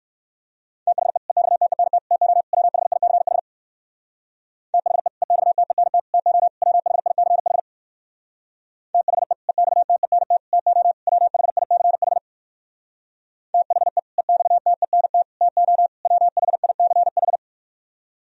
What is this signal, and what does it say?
Morse code, heard as a keyed tone, about 66 wpm, 700 Hz. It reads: THE EXTENT TO WHICH THE EXTENT TO WHICH THE EXTENT TO WHICH THE EXTENT TO WHICH